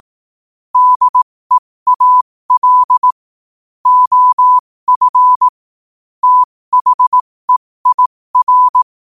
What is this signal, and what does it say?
Morse code, heard as a keyed tone, about 18 wpm, spaced as for 15 wpm, 1000 Hz. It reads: DEAL OF THEIR
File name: qotdfiles / 15wpm